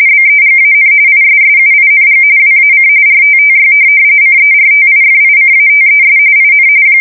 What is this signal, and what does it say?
RTTY, about 45.45 baud. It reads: BEGAN ISSUING WAC RTTY CERTIFICATES IN
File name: fsk170 / teletype-RTTY